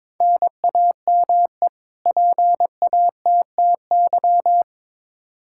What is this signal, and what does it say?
Morse code, heard as a keyed tone, about 22 wpm, 700 Hz. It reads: NAME PATTY
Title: K1USN SST QSO Practice 22wpm